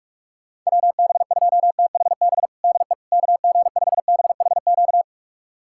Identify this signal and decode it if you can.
Morse code, heard as a keyed tone, about 46 wpm, 700 Hz. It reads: WB2THB DE KG5BHQ
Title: USA Callsign DE Callsign 45wpm